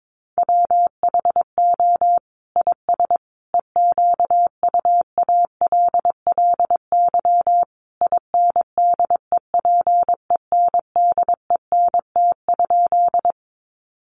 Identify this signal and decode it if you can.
Morse code, heard as a keyed tone, about 22 wpm, 700 Hz. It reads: WHO IS EQUALLY INDEPENDENT?
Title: Sentences from Top 1400 Words 22wpm